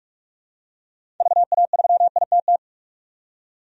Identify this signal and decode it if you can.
Morse code, heard as a keyed tone, about 46 wpm, 700 Hz. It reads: VA3ITT
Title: DXCC-100 Call Signs - 6 Characters - 45wpm